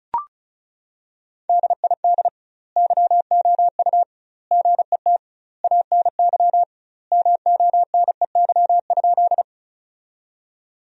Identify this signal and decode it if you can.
Morse code, heard as a keyed tone, about 34 wpm, 700 Hz. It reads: DID YOU GET ANY MONEY?
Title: 2x Word-Spacing - Sentences from Top 400 Words - 35wpm